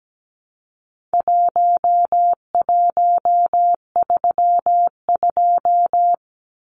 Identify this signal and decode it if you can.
Morse code, heard as a keyed tone, about 17 wpm, 700 Hz. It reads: 1132
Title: Numbers - Contest Style - 4-Digit 17wpm